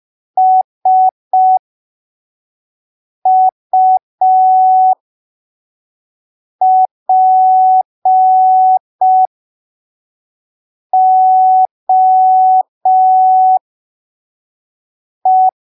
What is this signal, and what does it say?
Morse code, heard as a keyed tone, about 5 wpm, spaced as for 3 wpm, 750 Hz. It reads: SUPOE
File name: arquivos / texto / texto 26 05ppm